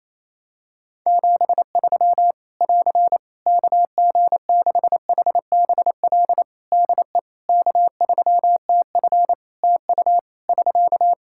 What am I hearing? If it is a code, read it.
Morse code, heard as a keyed tone, about 28 wpm, 700 Hz. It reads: 73 <AR> KG6HBL DE K3TF TU <SK>
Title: Practice QSO - Exchange 4 - Ending - 28wpm